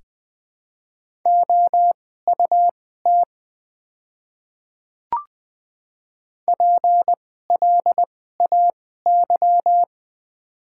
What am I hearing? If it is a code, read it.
Morse code, heard as a keyed tone, about 20 wpm, spaced as for 14 wpm, 700 Hz. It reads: OUT PLAY
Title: 2x Character Spacing - Top 200 Words - 20wpm